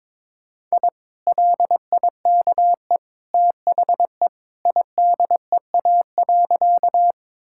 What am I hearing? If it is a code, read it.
Morse code, heard as a keyed tone, about 22 wpm, 700 Hz. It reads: I LIKE THE IDEA.
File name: practice / 22wpm / Sentences with 2-Word Phrases - 22wpm